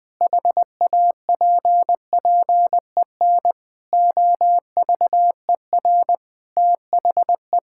Morse code, heard as a keyed tone, about 20 wpm, 700 Hz. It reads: HAPPEN OVER THE
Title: Sentences with 5-Word Phrases - 20wpm